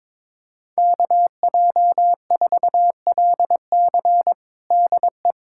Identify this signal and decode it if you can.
Morse code, heard as a keyed tone, about 22 wpm, 700 Hz. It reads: KJ4LC DE
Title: USA Callsign DE Callsign - 5 Characters - 22wpm